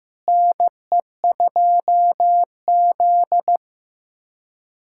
Morse code, heard as a keyed tone, about 15 wpm, 700 Hz. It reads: NE2Z
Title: USA Callsign DE Callsign - 4 Characters - 15wpm